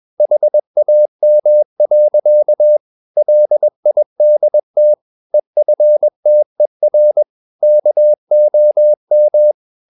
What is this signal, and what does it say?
Morse code, heard as a keyed tone, about 21 wpm, 600 Hz. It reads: HAM. LIDT EFTER KOM
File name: Ferie QRS 21wpm